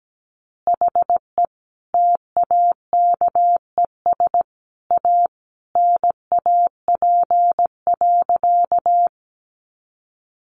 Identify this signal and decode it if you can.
Morse code, heard as a keyed tone, about 17 wpm, 700 Hz. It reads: HE TAKES A NAP.
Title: Easy Phrases 17wpm